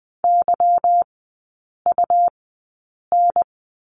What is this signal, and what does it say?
Morse code, heard as a keyed tone, about 20 wpm, spaced as for 8 wpm, 690 Hz. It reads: YUN
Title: 145 Lesson 24e - Letter y - Three Characters 20wpm